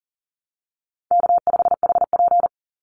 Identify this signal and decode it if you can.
Morse code, heard as a keyed tone, about 40 wpm, 700 Hz. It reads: K5HP
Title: Call Signs Repeated 3x - 40wpm